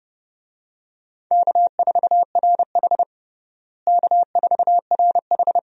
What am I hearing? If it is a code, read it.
Morse code, heard as a keyed tone, about 29 wpm, 700 Hz. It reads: K4RH K4RH
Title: USA Call Signs - 4 Characters - Repeated 2x - 30wpm